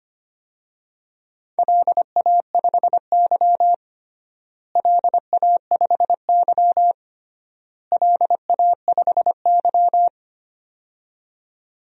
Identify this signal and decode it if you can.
Morse code, heard as a keyed tone, about 25 wpm, 700 Hz. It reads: LA5Y LA5Y LA5Y
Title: DXCC-100 Call Signs - 4 Characters - Repeated 3x - 25wpm